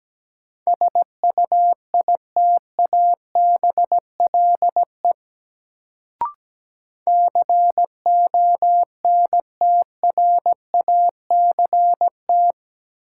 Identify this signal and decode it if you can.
Morse code, heard as a keyed tone, about 17 wpm, 700 Hz. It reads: SUITABLE CONTRACT
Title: Sentences from Top 1400 Words - New Word Drill - 17wpm